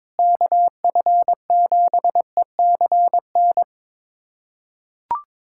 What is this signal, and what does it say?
Morse code, heard as a keyed tone, about 22 wpm, 700 Hz. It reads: KF7ECN
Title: USA Callsign DE Callsign - 6 Characters - 22wpm